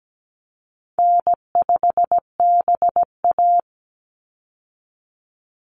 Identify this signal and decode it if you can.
Morse code, heard as a keyed tone, about 17 wpm, 700 Hz. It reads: N5BA
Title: Easy Call Signs - 17wpm